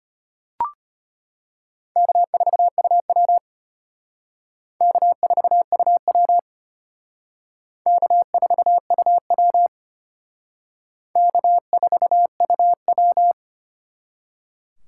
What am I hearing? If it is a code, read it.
Morse code, heard as a keyed tone, about 32 wpm, 700 Hz. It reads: K4UW K4UW K4UW K4UW